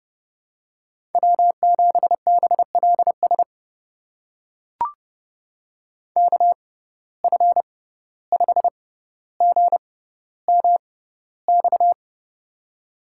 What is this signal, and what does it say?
Morse code, heard as a keyed tone, about 30 wpm, spaced as for 15 wpm, 700 Hz. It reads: W7BLS KF5GMX
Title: USA Call Signs - 6x Character Spacing - 30wpm